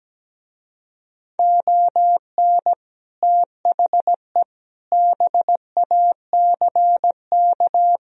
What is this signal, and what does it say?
Morse code, heard as a keyed tone, about 17 wpm, 700 Hz. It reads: ON THE BACK
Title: Speed-Racing - Sets of 4 Words 17wpm